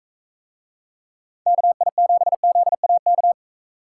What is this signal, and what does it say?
Morse code, heard as a keyed tone, about 42 wpm, 700 Hz. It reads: KI7ZAK